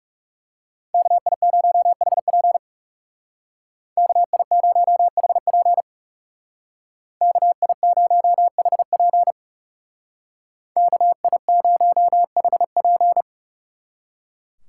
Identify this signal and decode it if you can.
Morse code, heard as a keyed tone, about 38 wpm, 700 Hz. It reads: KI0HP KI0HP KI0HP KI0HP